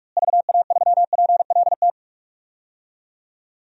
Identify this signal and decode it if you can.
Morse code, heard as a keyed tone, about 45 wpm, 700 Hz. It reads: VA3PLT
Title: DXCC-100 Call Signs - 6 Characters - Repeated 3x - 45wpm